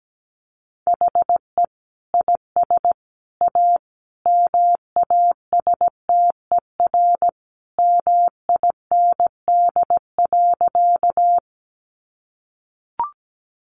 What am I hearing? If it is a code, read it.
Morse code, heard as a keyed tone, about 17 wpm, 700 Hz. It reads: HE IS A MASTER MIND.
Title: Sentences from Top 1300 Words 17wpm